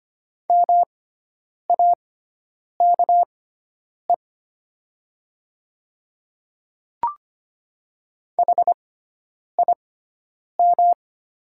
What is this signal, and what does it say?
Morse code, heard as a keyed tone, about 25 wpm, spaced as for 9 wpm, 700 Hz. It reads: MAKE HIM
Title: 6x Character Spacing - Top 100 Words - 25wpm